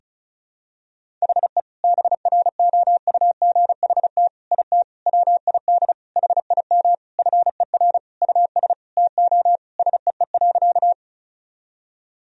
Morse code, heard as a keyed tone, about 36 wpm, 700 Hz. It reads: HE BROUGHT IT WID HIM FER US TO SEE.
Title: Sentences from Top 500 Words and Abbreviations 35wpm